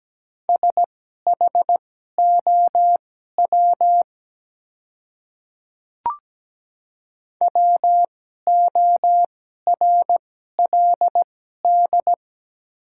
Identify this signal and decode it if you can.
Morse code, heard as a keyed tone, about 17 wpm, spaced as for 12 wpm, 700 Hz. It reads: SHOW WORLD
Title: 2x Character Spacing - Top 200 Words - 17wpm